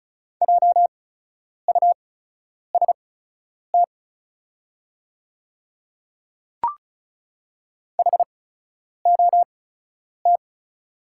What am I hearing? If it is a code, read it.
Morse code, heard as a keyed tone, about 36 wpm, spaced as for 10 wpm, 690 Hz. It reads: JUST HOT
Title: 8x Character Spacing - Top 300 Words - 35wpm